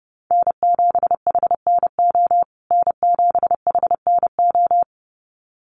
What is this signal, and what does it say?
Morse code, heard as a keyed tone, about 29 wpm, 700 Hz. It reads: N7HNO N7HNO
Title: USA Call Signs Repeated 2x 30wpm